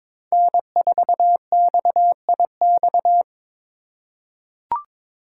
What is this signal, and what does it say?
Morse code, heard as a keyed tone, about 22 wpm, 700 Hz. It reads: N4XIX
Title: USA Call Signs - 3x Character Spacing - 22wpm